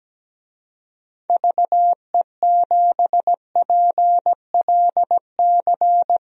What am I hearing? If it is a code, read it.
Morse code, heard as a keyed tone, about 17 wpm, 700 Hz. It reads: VE7PLC
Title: DXCC-100 Call Signs - 6 Characters - 17wpm